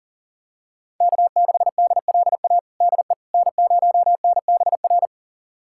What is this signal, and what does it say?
Morse code, heard as a keyed tone, about 40 wpm, 700 Hz. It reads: K6DLA DE N0NBR
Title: USA Callsign DE Callsign - 5 Characters - 40wpm